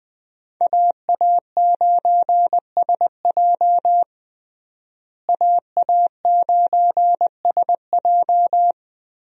Morse code, heard as a keyed tone, about 21 wpm, 700 Hz. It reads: AA9SJ AA9SJ
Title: USA Call Signs - 5 Characters - Repeated 3x - 20wpm